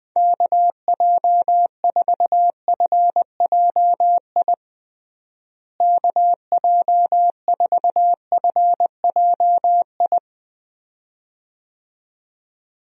Morse code, heard as a keyed tone, about 21 wpm, 700 Hz. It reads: KJ4FJI KJ4FJI
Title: USA Call Signs - 6 Characters - Repeated 2x - 20wpm